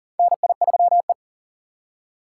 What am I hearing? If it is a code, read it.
Morse code, heard as a keyed tone, about 40 wpm, 700 Hz. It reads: NI3E
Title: USA Call Signs - 4 Characters - 6x Character Spacing - 40wpm